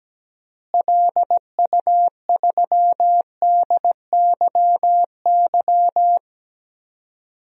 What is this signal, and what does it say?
Morse code, heard as a keyed tone, about 17 wpm, 700 Hz. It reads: LU3DYY
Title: DXCC-100 Call Signs - 6 Characters - Repeated 2x - 17wpm